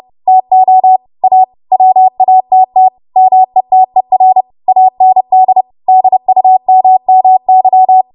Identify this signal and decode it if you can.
Morse code, heard as a keyed tone, about 30 wpm, 750 Hz. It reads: TO A WATT METER AND DUMMY
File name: Archive / 30 WPM / 30WPM